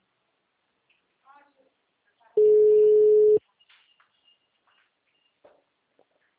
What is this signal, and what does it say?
Ringback tone (Europe, 425 Hz), with speech.